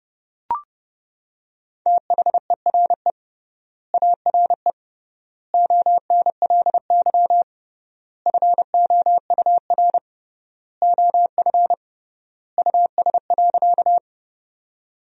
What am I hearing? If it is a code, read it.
Morse code, heard as a keyed tone, about 32 wpm, 700 Hz. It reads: THERE ARE ONLY FOUR OF US.
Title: 3x Word-Spacing - Sentences from Top 300 Words - 30wpm